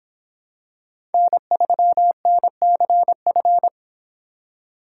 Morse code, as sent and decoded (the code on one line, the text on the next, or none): -. ...-- -. -.-. ..-.
N3NCF